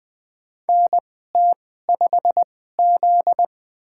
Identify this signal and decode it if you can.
Morse code, heard as a keyed tone, about 20 wpm, spaced as for 15 wpm, 700 Hz. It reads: NT5Z
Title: USA Call Signs - 4 Characters - 2x Character Spacing - 20wpm